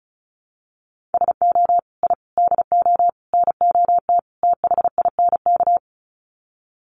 Morse code, heard as a keyed tone, about 34 wpm, 700 Hz. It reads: SO I DO NOT THINK
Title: Sets of 5 Words 35wpm